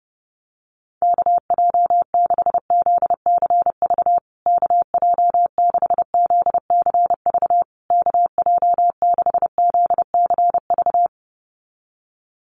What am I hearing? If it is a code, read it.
Morse code, heard as a keyed tone, about 30 wpm, 700 Hz. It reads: KJ6ZCV KJ6ZCV KJ6ZCV
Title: USA Call Signs Repeated 3x 30wpm